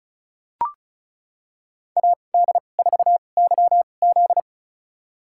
Morse code, heard as a keyed tone, about 34 wpm, spaced as for 25 wpm, 700 Hz. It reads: AD4YZ
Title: USA Call Signs - 5 Characters - 2x Character Spacing - 35wpm